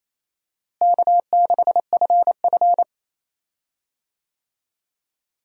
Morse code, heard as a keyed tone, about 28 wpm, 700 Hz. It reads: K6FF
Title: USA Call Signs - 4 Characters - Repeated 2x - 28wpm